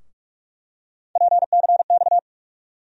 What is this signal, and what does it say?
Morse code, heard as a keyed tone, about 44 wpm, 700 Hz. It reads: PCX
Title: USA Call Sign Postfixes - 3 Letters 45wpm